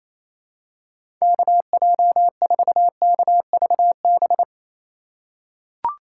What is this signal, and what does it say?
Morse code, heard as a keyed tone, about 27 wpm, 700 Hz. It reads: KJ4KVB
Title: USA Call Signs - 3x Character Spacing - 28wpm